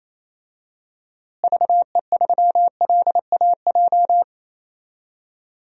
Morse code, heard as a keyed tone, about 28 wpm, 700 Hz. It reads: VE3LAJ
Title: DXCC-100 Call Signs - 6 Characters - Repeated 2x - 28wpm